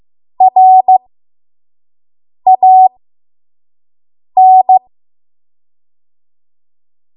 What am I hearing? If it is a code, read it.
Morse code, heard as a keyed tone, about 15 wpm, spaced as for 5 wpm, 750 Hz. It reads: RAN
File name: Morse / 05WPM